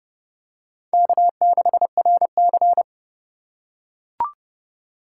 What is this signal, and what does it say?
Morse code, heard as a keyed tone, about 30 wpm, 700 Hz. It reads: K6RC
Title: K1USN SST QSO Practice - 30wpm